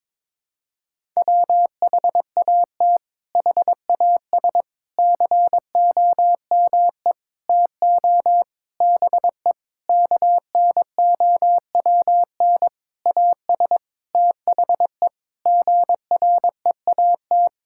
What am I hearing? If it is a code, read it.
Morse code, heard as a keyed tone, about 22 wpm, 700 Hz. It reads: WHAT HAS COME TO BE KNOWN AS THE GREAT
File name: practice / 22wpm / Sentences with 8-Word Phrases - 22wpm